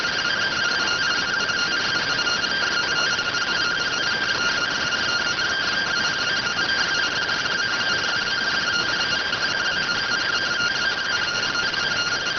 Two-tone teleprinter signal, about 170 Hz shift.